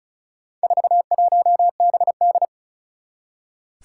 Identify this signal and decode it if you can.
Morse code, heard as a keyed tone, about 36 wpm, 700 Hz. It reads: 41BD
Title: Four Challenging Letter-Number Combinations 35wpm